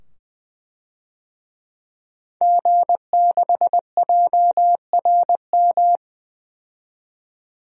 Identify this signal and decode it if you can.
Morse code, heard as a keyed tone, about 20 wpm, 700 Hz. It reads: G6JRM
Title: DXCC-100 Call Signs - 5 Characters - 20wpm